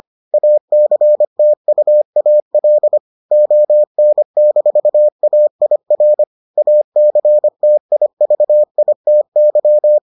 Morse code, heard as a keyed tone, about 25 wpm, 600 Hz. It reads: ACTUAL ON-AIR ACTIVITY